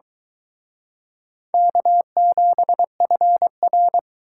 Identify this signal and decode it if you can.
Morse code, heard as a keyed tone, about 23 wpm, 700 Hz. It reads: K7FR